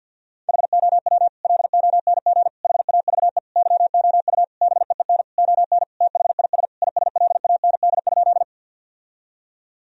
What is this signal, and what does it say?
Morse code, heard as a keyed tone, about 50 wpm, 700 Hz. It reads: HOW LONG HAVE YOU BEEN ON THIS ISLAND?
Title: Sentences from Top 1000 Words - Review Part 3 - 50wpm